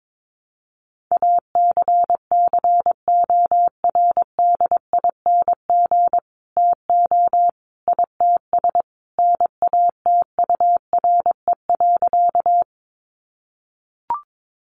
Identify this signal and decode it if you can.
Morse code, heard as a keyed tone, about 22 wpm, 700 Hz. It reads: ACCORDING TO ITS NATURE.
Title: Sentences from Top 700 Words 22wpm